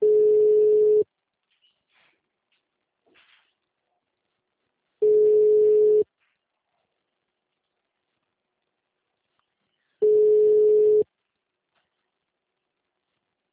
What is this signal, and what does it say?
Ringback tone (Europe, 425 Hz).